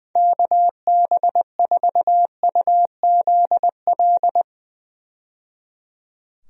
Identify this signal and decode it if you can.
Morse code, heard as a keyed tone, about 20 wpm, 700 Hz. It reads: KB4UZL
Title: USA Call Signs - 6 Characters - 20wpm